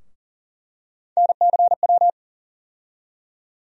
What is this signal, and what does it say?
Morse code, heard as a keyed tone, about 40 wpm, 700 Hz. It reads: NCW
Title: USA Call Sign Postfixes - 3 Letters 40wpm